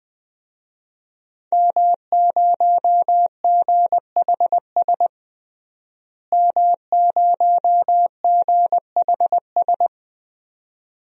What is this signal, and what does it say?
Morse code, heard as a keyed tone, about 20 wpm, 700 Hz. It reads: M0GHS M0GHS
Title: DXCC-100 Call Signs - 5 Characters - Repeated 2x - 20wpm